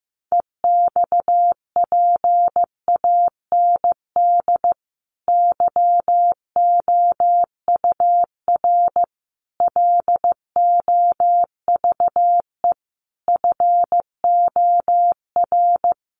Morse code, heard as a keyed tone, about 15 wpm, 700 Hz. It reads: EXPAND YOUR LOVE FOR
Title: Sentences from Top 1400 Words 15wpm